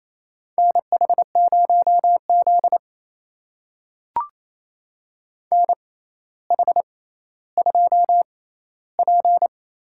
Morse code, heard as a keyed tone, about 28 wpm, spaced as for 14 wpm, 700 Hz. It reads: NH0Z NH2P
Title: USA Call Signs - 4 Characters - 6x Character Spacing - 28wpm